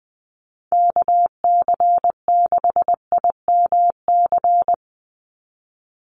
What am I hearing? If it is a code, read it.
Morse code, heard as a keyed tone, about 20 wpm, 700 Hz. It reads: KC6IMC
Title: USA Call Signs 20wpm